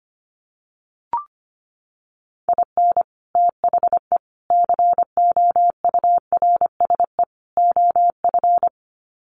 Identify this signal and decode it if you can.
Morse code, heard as a keyed tone, about 25 wpm, 700 Hz. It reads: IN THE COURSE OF
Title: Sets of 4 Words 25wpm